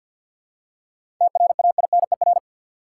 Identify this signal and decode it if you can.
Morse code, heard as a keyed tone, about 53 wpm, 700 Hz. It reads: TRAINER
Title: Seven Letter Words - 50wpm